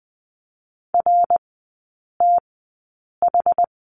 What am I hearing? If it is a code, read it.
Morse code, heard as a keyed tone, about 20 wpm, spaced as for 8 wpm, 700 Hz. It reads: RTH
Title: 074 Lesson 13e - Letter L - Three Characters 20wpm